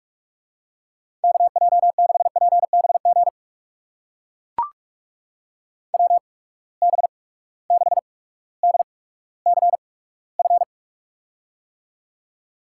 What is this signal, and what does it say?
Morse code, heard as a keyed tone, about 44 wpm, spaced as for 19 wpm, 700 Hz. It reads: KJ6PBG WB6DCF
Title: USA Call Signs - 8x Character Spacing - 45wpm